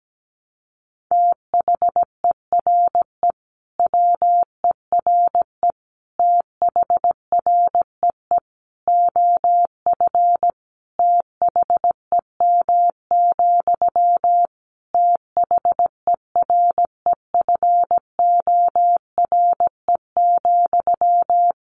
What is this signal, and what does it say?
Morse code, heard as a keyed tone, about 17 wpm, 700 Hz. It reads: THERE WERE THREE OF THEM, THEREFORE,
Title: Sentences from Top 800 Words 17wpm